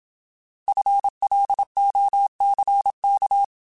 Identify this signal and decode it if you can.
Morse code, heard as a keyed tone, about 27 wpm, 800 Hz. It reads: FLOCK
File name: Alphabetical / Alphabetical mp3s / Fs @25